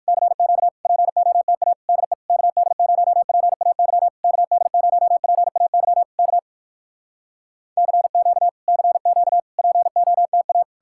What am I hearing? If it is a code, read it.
Morse code, heard as a keyed tone, about 59 wpm, 700 Hz. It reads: CQ POTA DE KD0PAY KD0PAY K CQ CQ POTA